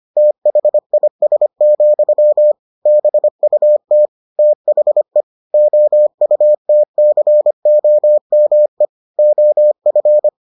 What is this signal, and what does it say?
Morse code, heard as a keyed tone, about 25 wpm, 600 Hz. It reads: THIS, BUT THE OUTCOME OF